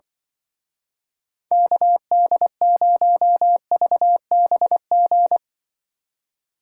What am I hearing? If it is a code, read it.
Morse code, heard as a keyed tone, about 23 wpm, 700 Hz. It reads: KD0VBG
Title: Speed Racing - USA 2x3 Call Signs 20wpm